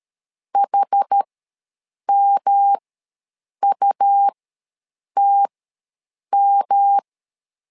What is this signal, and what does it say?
Morse code, heard as a keyed tone, about 13 wpm, spaced as for 7 wpm, 800 Hz. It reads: HMUTM
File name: LEZIONI / 3 MWRO / 3 M ws 12wpm 1745